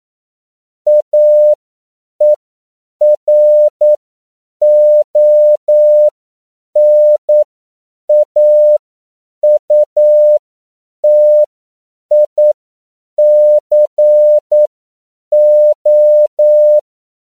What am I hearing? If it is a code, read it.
Morse code, heard as a keyed tone, about 9 wpm, spaced as for 7 wpm, 600 Hz. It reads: AERONAUTICO
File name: CW / 10PPM